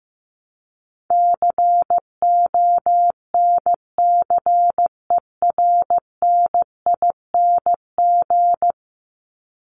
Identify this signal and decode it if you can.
Morse code, heard as a keyed tone, about 15 wpm, 700 Hz. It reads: CONCERNING
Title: ING Suffix - Top 100 Words with Base Words 15wpm